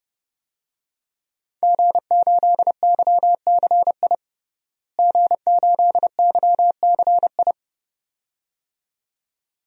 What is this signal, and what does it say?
Morse code, heard as a keyed tone, about 32 wpm, 700 Hz. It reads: G8YCI G8YCI
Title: DXCC-100 Call Signs - 5 Characters - Repeated 2x - 30wpm